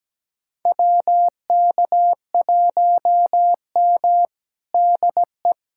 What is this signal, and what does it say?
Morse code, heard as a keyed tone, about 17 wpm, 700 Hz. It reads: WK1M DE
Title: USA Callsign DE Callsign - 4 Characters - 17wpm